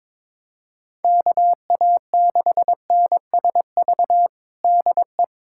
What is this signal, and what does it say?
Morse code, heard as a keyed tone, about 22 wpm, 700 Hz. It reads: KA6NSV DE
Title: USA Callsign DE Callsign 22wpm